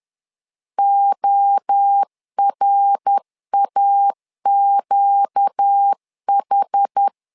Morse code, heard as a keyed tone, about 10 wpm, 800 Hz. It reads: ORAQH
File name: LEZIONI / 4 Q 10wpm 2210